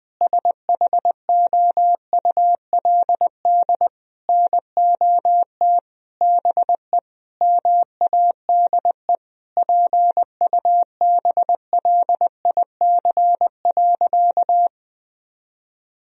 Morse code, heard as a keyed tone, about 20 wpm, 700 Hz. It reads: SHOULD NOT BE MADE PUBLIC.
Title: Sentences from Top 1000 Words - Review Part 3 - 20wpm